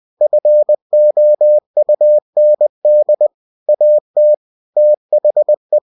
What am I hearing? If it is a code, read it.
Morse code, heard as a keyed tone, about 20 wpm, 600 Hz. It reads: FOUND AT THE